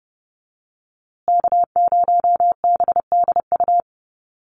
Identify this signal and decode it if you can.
Morse code, heard as a keyed tone, about 30 wpm, 700 Hz. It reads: K0BDU
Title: USA Call Signs Repeated 3x 30wpm